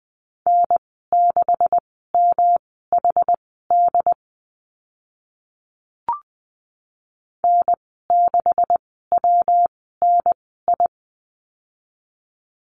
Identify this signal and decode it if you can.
Morse code, heard as a keyed tone, about 20 wpm, spaced as for 15 wpm, 700 Hz. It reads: N6MHD N6WNI
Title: 128 Lesson 21g - Number 6 - Call Signs 20wpm